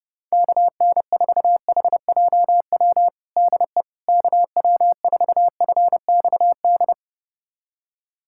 Morse code, heard as a keyed tone, about 29 wpm, 700 Hz. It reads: KN4HJW DE KW4FXD
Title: USA Callsign DE Callsign - 6 Characters - 30wpm